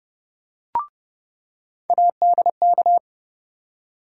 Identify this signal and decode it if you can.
Morse code, heard as a keyed tone, about 32 wpm, 700 Hz. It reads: ADK
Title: USA Call Sign Postfixes - 3 Letters 30wpm